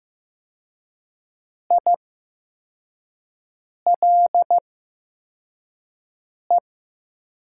Morse code, heard as a keyed tone, about 15 wpm, spaced as for 4 wpm, 700 Hz. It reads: ILE
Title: 8x Character Spacing - Top 500 Words - 15wpm